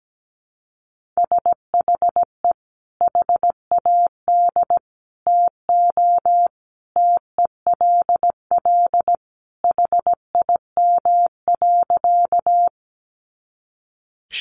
Morse code, heard as a keyed tone, about 17 wpm, 700 Hz. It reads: SHE HAD TO TELL HIM.
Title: Sentences from Top 200 Words 17wpm